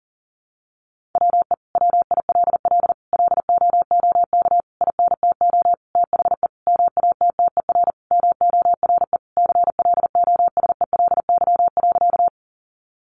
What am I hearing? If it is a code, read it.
Morse code, heard as a keyed tone, about 40 wpm, 700 Hz. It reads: WE WILL LOOK INTO THE MATTER MORE CLOSELY.
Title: Sentences from Top 1400 Words 40wpm